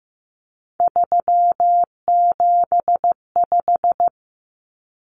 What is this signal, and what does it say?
Morse code, heard as a keyed tone, about 15 wpm, 700 Hz. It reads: 375
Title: Common Cut Numbers - 3-Digit 15wpm